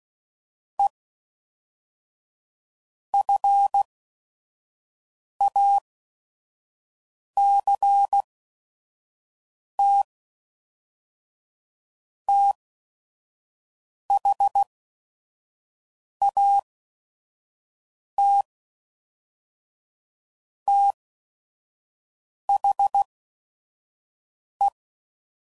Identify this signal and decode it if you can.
Morse code, heard as a keyed tone, about 16 wpm, spaced as for 5 wpm, 800 Hz. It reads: E FACT THAT THE
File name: CW Code Groups @5WPM / Technical Subjects / AND Gate